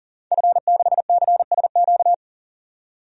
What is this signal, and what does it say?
Morse code, heard as a keyed tone, about 40 wpm, 700 Hz. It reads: F6CSQ
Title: DXCC-100 Call Signs - 5 Characters - 40wpm